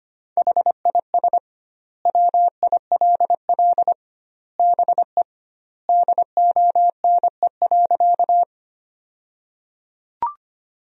Morse code, heard as a keyed tone, about 25 wpm, 700 Hz. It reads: HIS WILL BE DONE.
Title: 2x Word-Spacing - Sentences from Top 500 Words - 25wpm